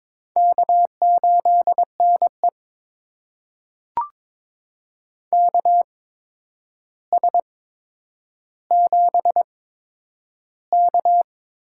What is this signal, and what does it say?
Morse code, heard as a keyed tone, about 22 wpm, spaced as for 9 wpm, 700 Hz. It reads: K8NE KS7K
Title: USA Call Signs - 4 Characters - 8x Character Spacing - 22wpm